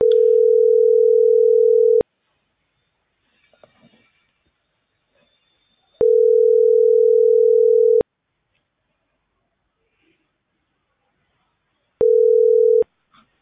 Ringback tone (North America), 440 + 480 Hz.